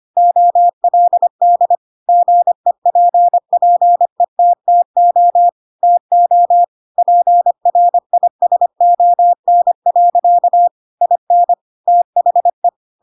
Morse code, heard as a keyed tone, about 25 wpm, 700 Hz. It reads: OLD GEPPETTO TO PRISON. IN THE
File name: Pinocchio / Pinocchio ch4 25 25 ik1hge